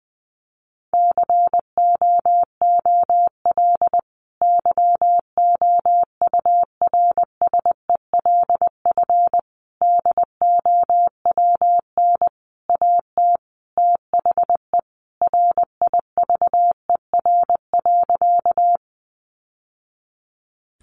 Morse code, heard as a keyed tone, about 20 wpm, 700 Hz. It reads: COOL YOURSELF DOWN AT THE RIVER.
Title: Sentences from Top 1200 Words 20wpm